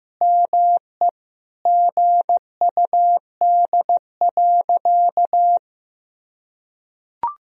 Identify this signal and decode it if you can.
Morse code, heard as a keyed tone, about 15 wpm, 700 Hz. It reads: ME GUD.
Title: Sentences from Top 200 Words and Abbreviations 15wpm